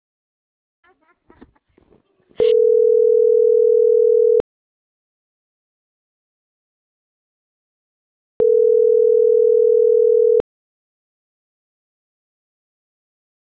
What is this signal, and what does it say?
Ringback tone (North America), 440 + 480 Hz.